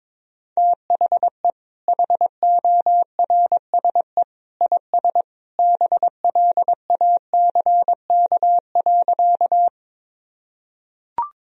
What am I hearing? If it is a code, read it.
Morse code, heard as a keyed tone, about 21 wpm, 700 Hz. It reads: THE HORSE IS BLACK.
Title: Sentences from Top 1000 Words - Review Part 2 - 22wpm